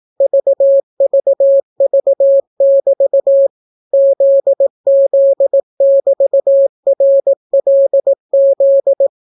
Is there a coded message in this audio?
Yes: Morse code from 0.2 to 9.1 s, heard as a keyed tone, about 18 wpm, 550 Hz. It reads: VVV<BT> ZZ<BT>RLZ